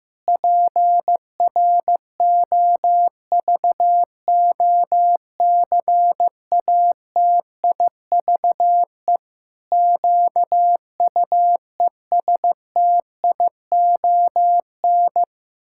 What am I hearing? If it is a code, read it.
Morse code, heard as a keyed tone, about 15 wpm, 700 Hz. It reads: PROVOCATIVE QUESTION
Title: Pangram Challenge v2 - 15wpm